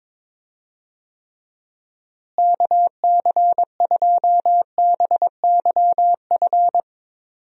Morse code, heard as a keyed tone, about 21 wpm, 700 Hz. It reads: KC2BYF
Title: USA Call Signs - 6 Characters - Repeated 3x - 22wpm